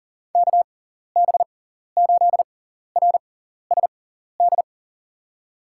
Morse code, heard as a keyed tone, about 39 wpm, spaced as for 14 wpm, 700 Hz. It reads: KB8RSD
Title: USA Call Signs - 6 Characters - 6x Character Spacing - 40wpm